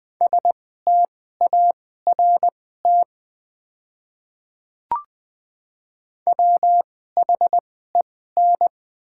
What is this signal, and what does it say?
Morse code, heard as a keyed tone, about 20 wpm, spaced as for 15 wpm, 700 Hz. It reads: START WHEN
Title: 2x Character Spacing - Top 500 Words - 20wpm